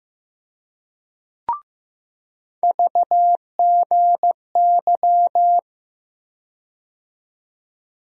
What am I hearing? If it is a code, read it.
Morse code, heard as a keyed tone, about 15 wpm, 700 Hz. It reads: VGY
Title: USA Call Sign Postfixes - 3 Letters 15wpm